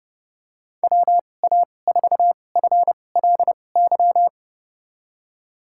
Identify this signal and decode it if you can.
Morse code, heard as a keyed tone, about 32 wpm, spaced as for 22 wpm, 700 Hz. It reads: WA4FLY
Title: USA Call Signs - 2x Character Spacing - 30wpm